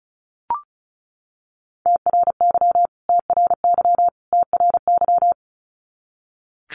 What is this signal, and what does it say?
Morse code, heard as a keyed tone, about 34 wpm, 700 Hz. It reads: TRY TRY TRY
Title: Top 200 Words Repeated 35wpm